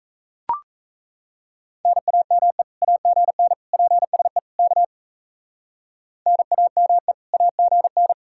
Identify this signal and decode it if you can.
Morse code, heard as a keyed tone, about 35 wpm, 700 Hz. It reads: NAME AGN PSE K NAME AGN